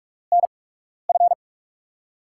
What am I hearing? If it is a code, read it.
Morse code, heard as a keyed tone, about 44 wpm, spaced as for 12 wpm, 700 Hz. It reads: NF
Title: USA Call Signs - 5 Characters - 8x Character Spacing - 45wpm